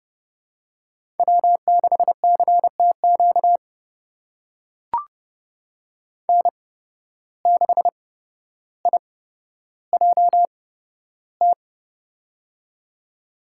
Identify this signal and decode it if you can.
Morse code, heard as a keyed tone, about 31 wpm, spaced as for 13 wpm, 700 Hz. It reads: W6CTQ N6IJT
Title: USA Call Signs - 5 Characters - 8x Character Spacing - 30wpm